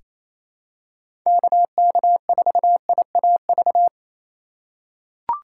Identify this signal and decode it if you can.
Morse code, heard as a keyed tone, about 28 wpm, 700 Hz. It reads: KK4IAV
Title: USA Call Signs - 3x Character Spacing - 28wpm